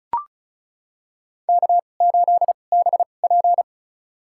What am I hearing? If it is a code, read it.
Morse code, heard as a keyed tone, about 34 wpm, spaced as for 25 wpm, 700 Hz. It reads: K8BP
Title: USA Call Signs - 2x Character Spacing - 35wpm